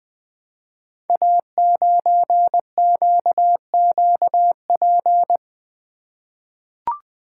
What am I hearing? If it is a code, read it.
Morse code, heard as a keyed tone, about 20 wpm, 700 Hz. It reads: A9QQP
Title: Code Groups - 5 Letter:Numbers - 20wpm